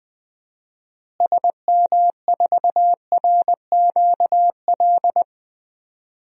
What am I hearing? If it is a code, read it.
Morse code, heard as a keyed tone, about 21 wpm, 700 Hz. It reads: SM4RQL
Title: DXCC-100 Call Signs - 6 Characters - Repeated 2x - 20wpm